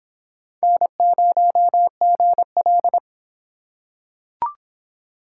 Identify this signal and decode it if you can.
Morse code, heard as a keyed tone, about 23 wpm, 700 Hz. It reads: N0GL